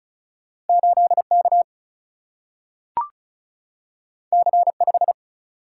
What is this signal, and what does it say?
Morse code, heard as a keyed tone, about 36 wpm, 700 Hz. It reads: 8K C5
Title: Single-Letter Single-Number 35wpm